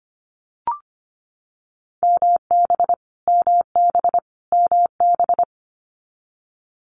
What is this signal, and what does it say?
Morse code, heard as a keyed tone, about 25 wpm, 700 Hz. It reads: MB MB MB
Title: Canadian Provinces and Territories - Repeated 3x - 25wpm